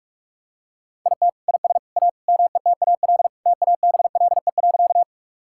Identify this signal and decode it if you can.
Morse code, heard as a keyed tone, about 42 wpm, 700 Hz. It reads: IT IS A METAL TABLE.
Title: Sentences from Top 1000 Words - Review Part 6 - 45wpm